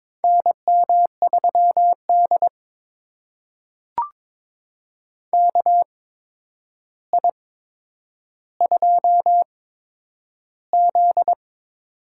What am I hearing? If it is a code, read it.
Morse code, heard as a keyed tone, about 22 wpm, spaced as for 9 wpm, 700 Hz. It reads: NM3D KI2Z
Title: USA Call Signs - 4 Characters - 8x Character Spacing - 22wpm